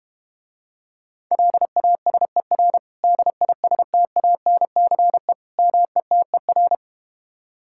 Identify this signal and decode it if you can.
Morse code, heard as a keyed tone, about 32 wpm, 700 Hz. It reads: LASER DISTANCE METER